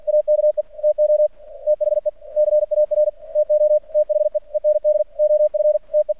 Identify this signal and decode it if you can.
Morse code, heard as a keyed tone, about 47 wpm, 600 Hz. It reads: AKE TO THE CAR TO THE END OF TE
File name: archived / CRSnet